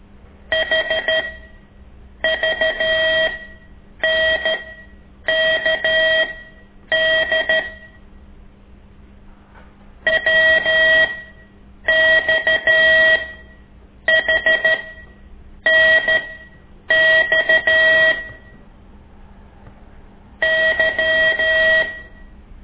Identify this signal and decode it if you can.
Morse code, heard as a keyed tone, about 13 wpm, spaced as for 7 wpm, 1900 Hz. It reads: HVNKD WXHNX Y
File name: ik1weg2 / CORSI CW